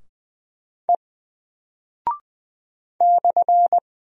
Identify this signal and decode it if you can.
Morse code, heard as a keyed tone, about 20 wpm, 700 Hz. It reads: E /
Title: Single Character (Letter:Number:Punctuation:Prosign) 20wpm